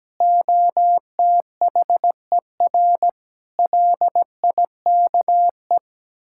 Morse code, heard as a keyed tone, about 17 wpm, 700 Hz. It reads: OTHER LIKE
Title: Sentences from Top 1000 Words - Review Part 6 - 17wpm